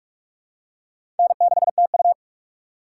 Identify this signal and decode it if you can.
Morse code, heard as a keyed tone, about 46 wpm, 700 Hz. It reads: N6TU
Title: USA Call Signs - 4 Characters - 2x Character Spacing - 45wpm